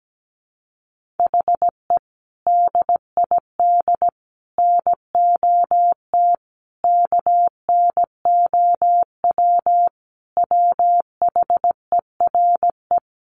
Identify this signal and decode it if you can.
Morse code, heard as a keyed tone, about 17 wpm, 700 Hz. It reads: HE DID NOT KNOW WHERE
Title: Sentences from Top 1500 Words 17wpm